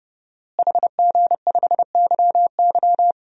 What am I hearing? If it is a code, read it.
Morse code, heard as a keyed tone, about 29 wpm, 700 Hz. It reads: HG5YY
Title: DXCC-100 Call Signs - 5 Characters - Repeated 3x - 30wpm